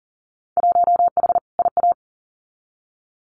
Morse code, heard as a keyed tone, about 40 wpm, 700 Hz. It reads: 1H IA
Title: Winter Field Day Exchanges 40wpm